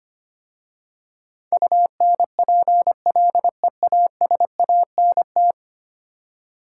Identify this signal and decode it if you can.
Morse code, heard as a keyed tone, about 25 wpm, 700 Hz. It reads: UNPLEASANT
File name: 10 Letter Words - First 3 Letters Given - 25wpm